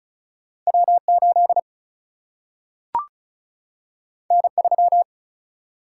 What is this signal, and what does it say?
Morse code, heard as a keyed tone, about 36 wpm, 700 Hz. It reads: W 8 N 3
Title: Single-Letter Single-Number 35wpm